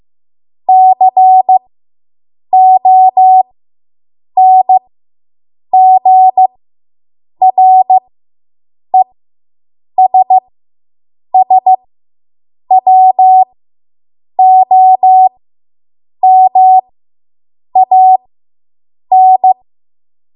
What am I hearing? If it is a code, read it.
Morse code, heard as a keyed tone, about 15 wpm, spaced as for 7 wpm, 750 Hz. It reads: CONGRESSWOMAN